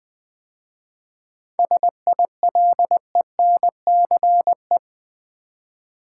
Morse code, heard as a keyed tone, about 20 wpm, 700 Hz. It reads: SILENCE
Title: 7 Letter Words - First 3 Letters Given - 20wpm